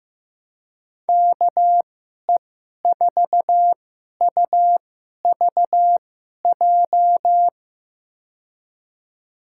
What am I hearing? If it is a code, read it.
Morse code, heard as a keyed tone, about 16 wpm, spaced as for 11 wpm, 700 Hz. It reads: KE4UVJ
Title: USA Call Signs - 6 Characters - 2x Character Spacing - 15wpm